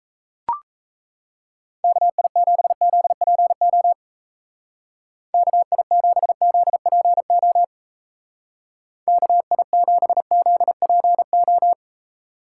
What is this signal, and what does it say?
Morse code, heard as a keyed tone, about 34 wpm, 700 Hz. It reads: KI7ZPO KI7ZPO KI7ZPO
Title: Speed Racing - USA Call Signs 28wpm